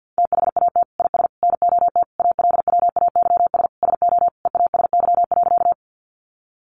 Morse code, heard as a keyed tone, about 50 wpm, 700 Hz. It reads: THAT IS NOT ALWAYS SO EASY.
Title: Sentences from Top 700 Words 50wpm